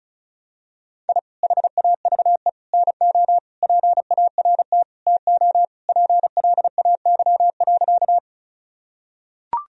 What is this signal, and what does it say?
Morse code, heard as a keyed tone, about 36 wpm, 700 Hz. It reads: I HAVE NO PART TO PLAY.
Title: Sentences from Top 1000 Words - Review Part 1 - 35wpm